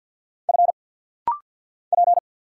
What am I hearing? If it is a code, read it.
Morse code, heard as a keyed tone, about 50 wpm, 690 Hz. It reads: F P
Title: Single Letters - Rapid-Fire 50wpm